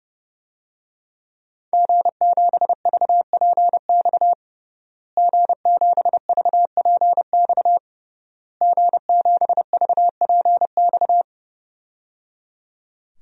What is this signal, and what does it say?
Morse code, heard as a keyed tone, about 30 wpm, 700 Hz. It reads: G7VPX G7VPX G7VPX
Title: DXCC-100 Call Signs - 5 Characters - Repeated 3x - 30wpm